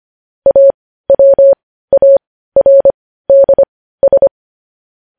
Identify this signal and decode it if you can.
Morse code, heard as a keyed tone, about 25 wpm, spaced as for 15 wpm, 550 Hz. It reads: AWARDS
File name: podcast / News headlines-1x-25x15